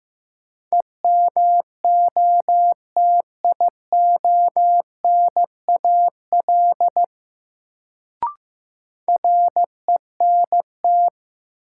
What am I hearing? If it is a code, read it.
Morse code, heard as a keyed tone, about 15 wpm, 700 Hz. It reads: EMOTIONAL RENT
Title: Sentences from Top 1000 Words - Review Part 5 - New Word Drill - 15wpm